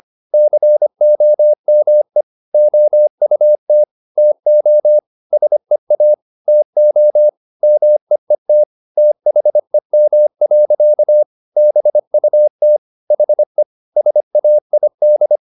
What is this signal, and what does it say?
Morse code, heard as a keyed tone, about 25 wpm, 600 Hz. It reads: COME OUT TO SEA TO MEET THEM. BUT HE SAID
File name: the old man and the sea part 1 25wpm